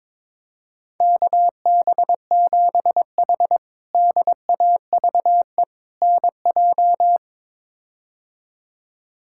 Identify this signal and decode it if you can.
Morse code, heard as a keyed tone, about 21 wpm, 700 Hz. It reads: KB7H DAVE NJ
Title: K1USN SST QSO Practice - 22wpm